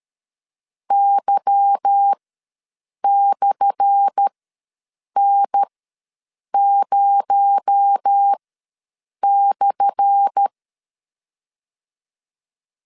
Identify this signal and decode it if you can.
Morse code, heard as a keyed tone, about 12 wpm, spaced as for 7 wpm, 800 Hz. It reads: Y/N0/
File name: LEZIONI / 10 SIGNS / 10A SLASH ws 12wpm 3443